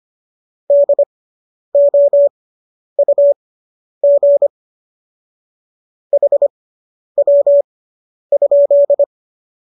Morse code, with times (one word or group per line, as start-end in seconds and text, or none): 0.7-4.5 DOUG
6.1-9.0 HW?